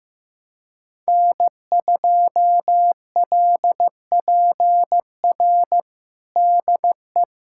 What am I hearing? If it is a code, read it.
Morse code, heard as a keyed tone, about 16 wpm, 700 Hz. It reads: N2LPR DE
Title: USA Callsign DE Callsign - 5 Characters - 15wpm